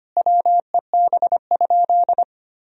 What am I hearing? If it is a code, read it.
Morse code, heard as a keyed tone, about 25 wpm, 700 Hz. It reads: WEB?
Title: Sentences from Top 1000 Words - Review Part 4 - 25wpm